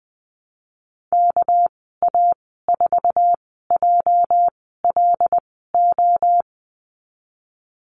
Morse code, heard as a keyed tone, about 20 wpm, spaced as for 14 wpm, 700 Hz. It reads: KA4JLO
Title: 231 Lesson 37g - Letter x - Call Signs 20wpm